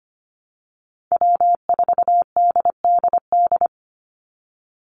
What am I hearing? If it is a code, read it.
Morse code, heard as a keyed tone, about 25 wpm, 700 Hz. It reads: W4DDD
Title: Call Signs - 25wpm